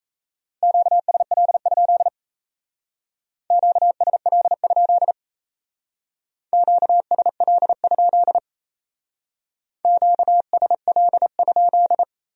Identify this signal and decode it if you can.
Morse code, heard as a keyed tone, about 35 wpm, 700 Hz. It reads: QSL? QSL? QSL? QSL?